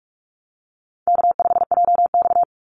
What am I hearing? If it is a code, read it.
Morse code, heard as a keyed tone, about 44 wpm, 700 Hz. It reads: K5JX
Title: Call Signs Repeated 2x - 45wpm